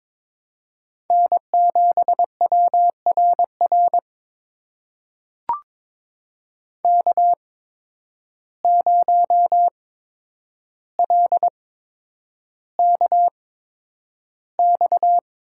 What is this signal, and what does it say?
Morse code, heard as a keyed tone, about 22 wpm, spaced as for 9 wpm, 700 Hz. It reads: N7WRR K0LKX
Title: USA Call Signs - 5 Characters - 8x Character Spacing - 22wpm